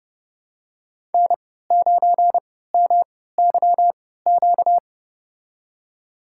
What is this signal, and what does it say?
Morse code, heard as a keyed tone, about 32 wpm, spaced as for 17 wpm, 700 Hz. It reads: N9MYQ
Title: USA Call Signs - 5 Characters - 3x Character Spacing - 30wpm